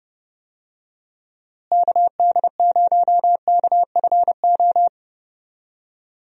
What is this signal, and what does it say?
Morse code, heard as a keyed tone, about 31 wpm, 700 Hz. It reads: KD0KFO
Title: USA Call Signs - 6 Characters - Repeated 3x - 30wpm